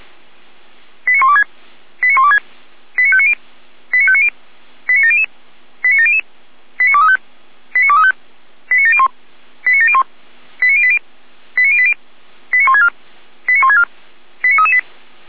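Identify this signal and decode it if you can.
Five-tone selective call (ZVEI), heard as quick runs of five tones, tones of about 70 ms. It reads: 89126 89126 89590 89590 89700 89700 89235 89235 89701 89701 89080 89080 89164 89164 89338
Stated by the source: ZVEI five-tone call received by SDR, feuerwehr-orlamuende.org fire brigade audio archive, code 89120.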